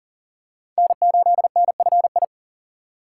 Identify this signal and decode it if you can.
Morse code, heard as a keyed tone, about 40 wpm, 700 Hz. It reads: N8NFI
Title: USA Call Signs - 5 Characters - 3x Character Spacing - 40wpm